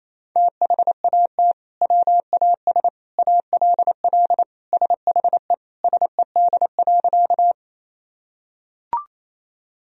Morse code, heard as a keyed tone, about 28 wpm, 700 Hz. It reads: THAT WAS ALL SHE SED.
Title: Sentences from Top 100 Words and Abbreviations 28wpm